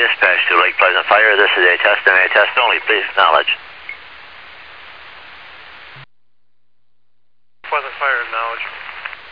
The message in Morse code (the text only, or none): none